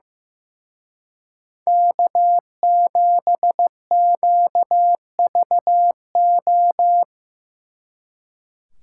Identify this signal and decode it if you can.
Morse code, heard as a keyed tone, about 18 wpm, 700 Hz. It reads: K7QVO
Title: Speed Racing - USA 1x3 Call Signs 15wpm